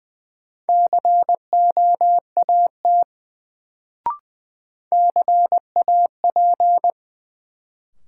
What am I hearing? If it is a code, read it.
Morse code, heard as a keyed tone, about 21 wpm, 700 Hz. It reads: COAT CAP
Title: Sentences from Top 1600 Words - New Word Drill Alt - 20wpm